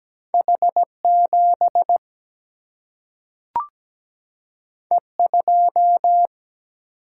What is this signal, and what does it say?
Morse code, heard as a keyed tone, about 17 wpm, 700 Hz. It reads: H 7 E 2
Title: Single-Letter Single-Number 17wpm